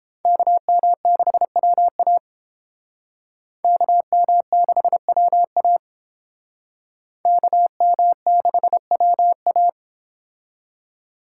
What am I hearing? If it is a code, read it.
Morse code, heard as a keyed tone, about 30 wpm, 700 Hz. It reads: KM6WA KM6WA KM6WA